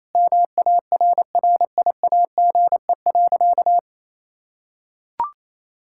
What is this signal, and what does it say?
Morse code, heard as a keyed tone, about 28 wpm, 700 Hz. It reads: MARRIAGE.
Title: Sentences from Top 1000 Words - Review Part 5 - 28wpm